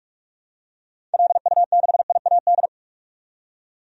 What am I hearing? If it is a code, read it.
Morse code, heard as a keyed tone, about 46 wpm, 700 Hz. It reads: LU6IAD
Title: DXCC-100 Call Signs - 6 Characters - 45wpm